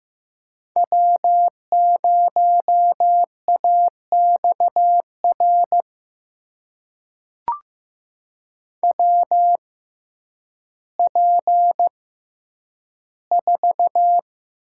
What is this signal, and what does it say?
Morse code, heard as a keyed tone, about 15 wpm, spaced as for 9 wpm, 700 Hz. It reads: W0AXR WP4
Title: USA Call Signs - 5 Characters - 6x Character Spacing - 15wpm